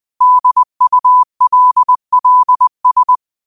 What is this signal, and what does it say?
Morse code, heard as a keyed tone, about 20 wpm, 1000 Hz. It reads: DULLS
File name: qotdfiles / 20wpm